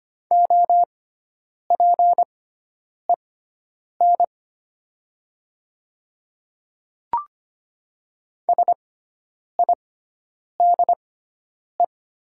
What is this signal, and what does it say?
Morse code, heard as a keyed tone, about 25 wpm, spaced as for 9 wpm, 700 Hz. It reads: OPEN SIDE
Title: 6x Character Spacing - Top 300 Words - 25wpm